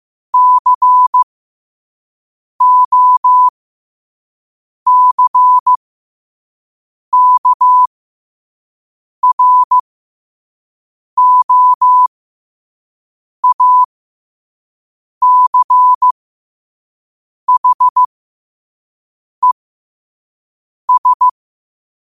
Morse code, heard as a keyed tone, about 15 wpm, spaced as for 5 wpm, 1000 Hz. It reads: COCKROACHES
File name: qotdfiles / 5wpm